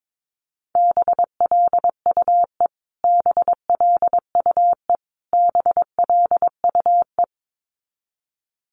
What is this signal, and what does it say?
Morse code, heard as a keyed tone, about 22 wpm, 700 Hz. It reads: BLUE BLUE BLUE
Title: Top 500 Words Repeated 22wpm